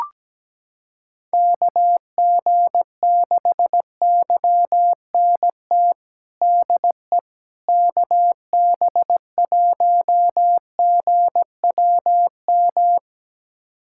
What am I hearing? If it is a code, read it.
Morse code, heard as a keyed tone, about 17 wpm, 700 Hz. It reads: KG6YNT DE KB1GWM
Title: USA Callsign DE Callsign - 6 Characters - 17wpm